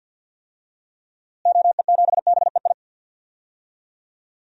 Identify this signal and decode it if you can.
Morse code, heard as a keyed tone, about 52 wpm, 700 Hz. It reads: OE7BEI
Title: DXCC-100 Call Signs - 6 Characters - Repeated 2x - 50wpm